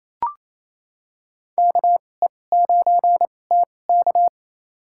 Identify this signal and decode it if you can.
Morse code, heard as a keyed tone, about 28 wpm, spaced as for 20 wpm, 700 Hz. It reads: KE9TK
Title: USA Call Signs - 2x Character Spacing - 28wpm